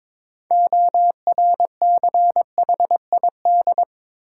Morse code, heard as a keyed tone, about 22 wpm, 700 Hz. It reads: ORCHID
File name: Difficult Words with Hints - 22wpm